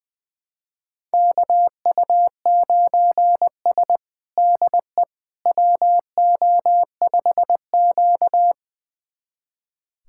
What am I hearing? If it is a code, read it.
Morse code, heard as a keyed tone, about 20 wpm, 700 Hz. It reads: KU9S DE WO5Q
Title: USA Callsign DE Callsign - 4 Characters - 20wpm